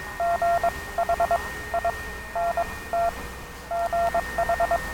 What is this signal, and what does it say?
Morse code, heard as a keyed tone, about 22 wpm, spaced as for 18 wpm, 690 Hz, with noise. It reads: GH I NT GH
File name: attachments / Gromkaja STN na TPP